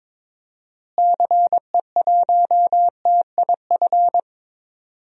Morse code, heard as a keyed tone, about 22 wpm, 700 Hz. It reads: CE1TIF
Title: DXCC-100 Call Signs - 6 Characters - Repeated 2x - 22wpm